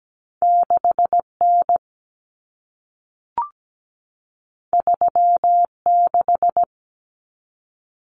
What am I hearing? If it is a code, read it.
Morse code, heard as a keyed tone, about 17 wpm, 700 Hz. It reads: 6N 36
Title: Common Cut Numbers - 2-Digit 17wpm